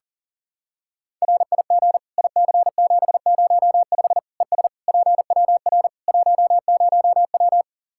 Morse code, heard as a keyed tone, about 40 wpm, 700 Hz. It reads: RIG IC705 ES PWR 10W
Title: Practice QSO - Exchange 2 - Info - 40wpm